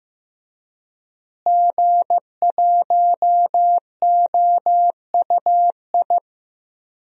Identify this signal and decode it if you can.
Morse code, heard as a keyed tone, about 15 wpm, 700 Hz. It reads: G1OUI
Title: DXCC-100 Call Signs - 5 Characters - Repeated 2x - 15wpm